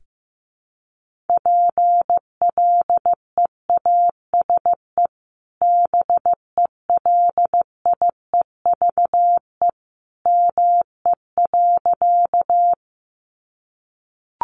Morse code, heard as a keyed tone, about 15 wpm, 700 Hz. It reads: PLEASE BELIEVE ME.
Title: Sentences from Top 1500 Words 15wpm